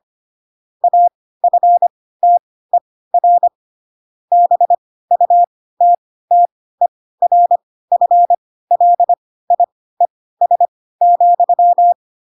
Morse code, heard as a keyed tone, about 25 wpm, spaced as for 16 wpm, 700 Hz. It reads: AFTER BUTTERFLIES,
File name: Pinocchio / Pinocchio ch4 10 25 ik1hge